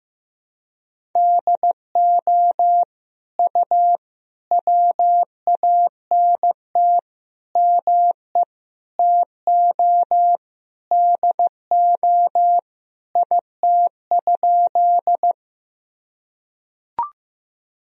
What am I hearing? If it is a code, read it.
Morse code, heard as a keyed tone, about 15 wpm, 700 Hz. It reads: DO U WANT ME TO DO IT?
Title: Sentences from Top 200 Words and Abbreviations 15wpm